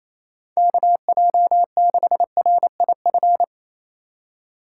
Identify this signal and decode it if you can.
Morse code, heard as a keyed tone, about 28 wpm, 700 Hz. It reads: KJ6RIF
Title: USA Call Signs - 6 Characters - 3x Character Spacing - 28wpm